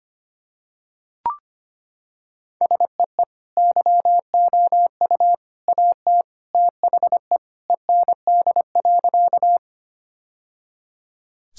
Morse code, heard as a keyed tone, about 25 wpm, 700 Hz. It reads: SEE YOU AT THE END.
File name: practice / 25wpm / Sentences with 3-Word Phrases - 25wpm